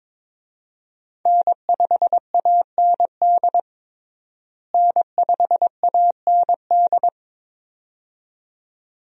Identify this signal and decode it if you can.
Morse code, heard as a keyed tone, about 21 wpm, 700 Hz. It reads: N5AND N5AND
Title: USA Call Signs - 5 Characters - Repeated 2x - 22wpm